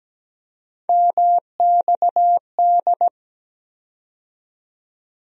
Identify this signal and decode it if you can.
Morse code, heard as a keyed tone, about 17 wpm, 700 Hz. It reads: MXD
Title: USA Call Sign Postfixes - 3 Letters 17wpm